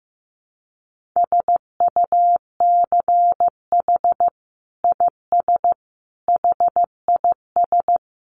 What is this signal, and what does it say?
Morse code, heard as a keyed tone, about 15 wpm, 700 Hz. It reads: SUCH IS HIS
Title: Sentences from Top 900 Words 15wpm